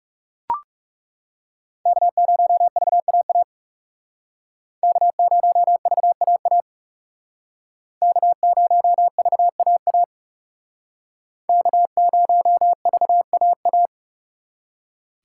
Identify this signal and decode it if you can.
Morse code, heard as a keyed tone, about 39 wpm, 700 Hz. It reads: K0VAA K0VAA K0VAA K0VAA